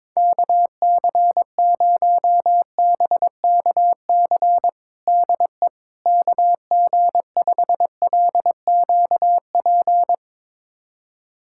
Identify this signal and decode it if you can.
Morse code, heard as a keyed tone, about 22 wpm, 700 Hz. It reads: KC0BKC DE KG5LQP
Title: USA Callsign DE Callsign - 6 Characters - 22wpm